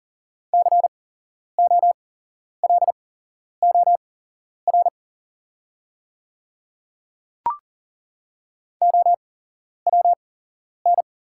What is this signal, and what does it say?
Morse code, heard as a keyed tone, about 40 wpm, spaced as for 11 wpm, 690 Hz. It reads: COLOR OWN
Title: 8x Character Spacing - Top 300 Words - 40wpm